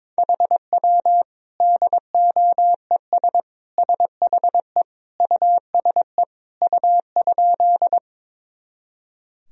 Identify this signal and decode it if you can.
Morse code, heard as a keyed tone, about 22 wpm, 700 Hz. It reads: HW DOES SHE USE U?
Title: Sentences from Top 200 Words and Abbreviations 22wpm